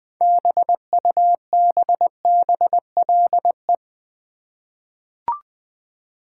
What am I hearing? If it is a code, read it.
Morse code, heard as a keyed tone, about 20 wpm, 700 Hz. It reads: BUBBLE
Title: 6 Letter Words - First 3 Letters Given - 20wpm